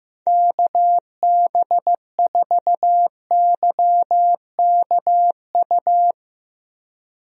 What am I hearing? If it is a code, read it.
Morse code, heard as a keyed tone, about 15 wpm, 700 Hz. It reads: KB4YKU
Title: USA Call Signs - 6 Characters - Repeated 2x - 15wpm